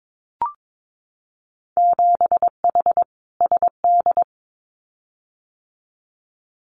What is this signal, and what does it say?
Morse code, heard as a keyed tone, about 22 wpm, 700 Hz. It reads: 7H SD
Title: Winter Field Day Exchanges Encore 22wpm